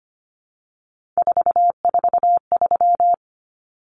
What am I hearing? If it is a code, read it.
Morse code, heard as a keyed tone, about 25 wpm, 700 Hz. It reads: 443
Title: Numbers - 3-Digit 25wpm